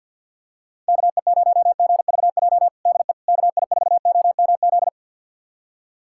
Morse code, heard as a keyed tone, about 52 wpm, 700 Hz. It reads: KE0GVJ DE KI4OMZ
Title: USA Callsign DE Callsign 50wpm